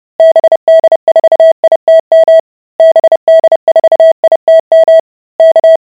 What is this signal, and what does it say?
Morse code, heard as a keyed tone, about 30 wpm, 650 Hz. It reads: BD4ITM BD4ITM K